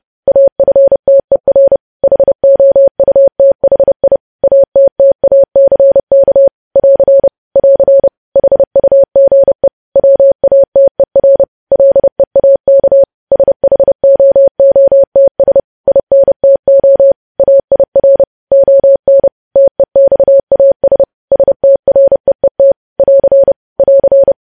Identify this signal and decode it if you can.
Morse code, heard as a keyed tone, about 30 wpm, 550 Hz. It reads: AFTER HOUTHI ATTACK <AR> <AR> HUGE WATER LEAK SHOOTS INTO AIR ON TEXAS STREET <AR> <AR>
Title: News Headlines at 30